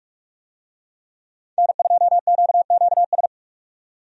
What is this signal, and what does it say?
Morse code, heard as a keyed tone, about 41 wpm, 700 Hz. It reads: N2QQS
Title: USA Call Signs - 5 Characters - 45wpm